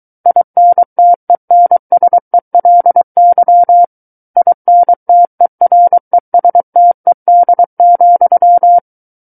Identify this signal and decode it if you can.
Morse code, heard as a keyed tone, about 23 wpm, 700 Hz. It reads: INTENSELY INTERESTED,
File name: Book One / 14-Chapter 14 - 23 WPM